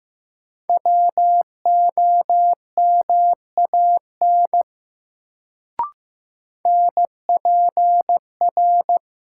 Morse code, heard as a keyed tone, about 16 wpm, 700 Hz. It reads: WOMAN NPR
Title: Sets of 5 Words - New Word Drill - 15wpm